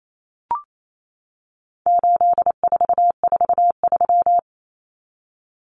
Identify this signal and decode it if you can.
Morse code, heard as a keyed tone, about 28 wpm, 700 Hz. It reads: 8443
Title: Common Cut Numbers - 4-Digit 28wpm